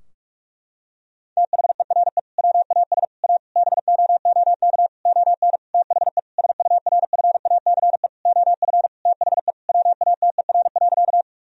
Morse code, heard as a keyed tone, about 41 wpm, 700 Hz. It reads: THERE WAS A BOOK ON THE SURFACE OF THE WATER.
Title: Sentences from Top 1000 Words - Review Part 3 - 45wpm